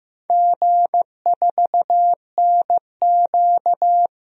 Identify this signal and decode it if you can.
Morse code, heard as a keyed tone, about 16 wpm, 700 Hz. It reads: G4NQ
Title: DXCC-100 Call Signs - 4 Characters - 15wpm